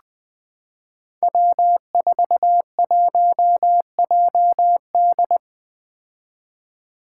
Morse code, heard as a keyed tone, about 20 wpm, 700 Hz. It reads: W41JD
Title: Five Challenging Letter-Number Combinations 20wpm